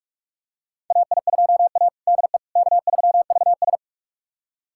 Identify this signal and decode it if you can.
Morse code, heard as a keyed tone, about 46 wpm, 690 Hz. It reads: AI2A DE K3VS
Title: USA Callsign DE Callsign - 4 Characters - 45wpm